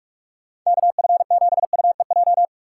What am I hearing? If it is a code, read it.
Morse code, heard as a keyed tone, about 46 wpm, 700 Hz. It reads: KF7UEJ
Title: USA Call Signs - 6 Characters - 2x Character Spacing - 45wpm